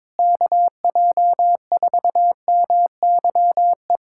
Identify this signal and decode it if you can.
Morse code, heard as a keyed tone, about 22 wpm, 700 Hz. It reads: KJ4MYE
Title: USA Callsign DE Callsign - 6 Characters - 22wpm